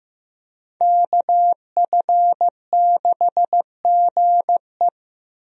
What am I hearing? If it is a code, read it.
Morse code, heard as a keyed tone, about 15 wpm, 700 Hz. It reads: KF6GE
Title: USA Call Signs - 5 Characters - Repeated 2x - 15wpm